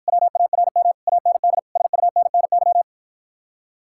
Morse code, heard as a keyed tone, about 54 wpm, 700 Hz. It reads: WARM AND SUNNY